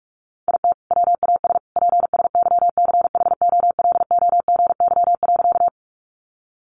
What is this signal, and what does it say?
Morse code, heard as a keyed tone, about 44 wpm, 700 Hz. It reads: IT WAS PSYCHOLOGY.
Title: Sentences from Top 1400 Words 45wpm